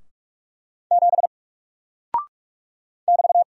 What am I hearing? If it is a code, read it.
Morse code, heard as a keyed tone, about 44 wpm, 700 Hz. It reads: 7 <BT>
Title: Single Character (Letter:Number:Punctuation:Prosign) Alt 45wpm